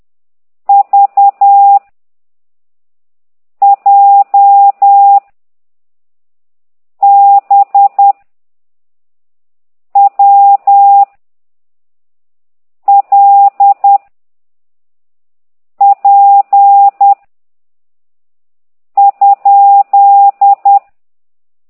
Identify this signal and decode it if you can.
Morse code, heard as a keyed tone, about 10 wpm, 800 Hz. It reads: V J B W L P ?